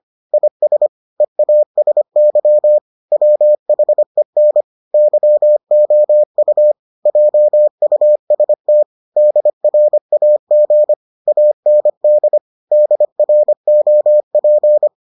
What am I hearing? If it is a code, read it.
Morse code, heard as a keyed tone, about 25 wpm, 600 Hz. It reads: IS EASY WHEN YOU JUST DRAG AND DROP